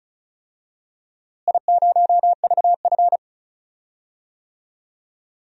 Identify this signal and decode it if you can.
Morse code, heard as a keyed tone, about 34 wpm, 700 Hz. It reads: I0VF
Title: DXCC-100 Call Signs - 4 Characters - Repeated 3x - 35wpm